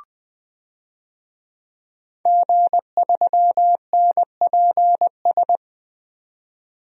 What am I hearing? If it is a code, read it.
Morse code, heard as a keyed tone, about 20 wpm, 700 Hz. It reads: G3NPS
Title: DXCC-100 Call Signs - 5 Characters - 20wpm